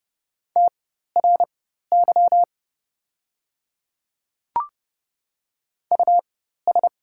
Morse code, heard as a keyed tone, about 30 wpm, spaced as for 14 wpm, 700 Hz. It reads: TRY US
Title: 4x Character Spacing - Top 200 Words - 30wpm